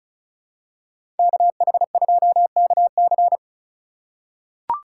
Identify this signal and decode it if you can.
Morse code, heard as a keyed tone, about 36 wpm, 700 Hz. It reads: KH2KC
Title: USA Call Signs - 5 Characters - 3x Character Spacing - 35wpm